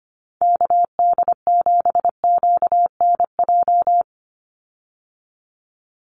Morse code, heard as a keyed tone, about 25 wpm, 700 Hz. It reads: KD7QNJ
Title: USA Call Signs 25wpm